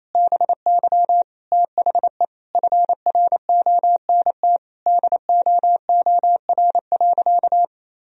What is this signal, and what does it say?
Morse code, heard as a keyed tone, about 28 wpm, 700 Hz. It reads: BY THE FRONT DOOR.
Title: Sentences from Top 1000 Words - Review Part 2 - 28wpm